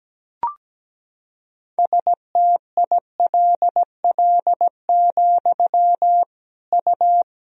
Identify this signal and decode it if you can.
Morse code, heard as a keyed tone, about 17 wpm, 700 Hz. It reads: STILL, U
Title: Sentences from Top 300 Words and Abbreviations 17wpm